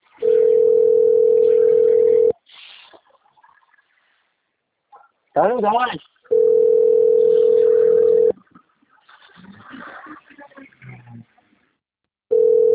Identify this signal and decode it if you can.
Ringback tone (North America), 440 + 480 Hz, with speech.
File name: rg-1600-unknown- - - .66207